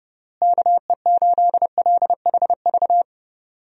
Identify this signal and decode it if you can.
Morse code, heard as a keyed tone, about 32 wpm, 700 Hz. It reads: KE8LHV
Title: USA Call Signs - 6 Characters - Repeated 3x - 30wpm